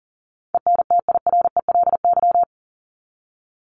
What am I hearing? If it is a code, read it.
Morse code, heard as a keyed tone, about 40 wpm, 700 Hz. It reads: ENTIRELY
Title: LY Suffix - Top 100 Words with Base Words 40wpm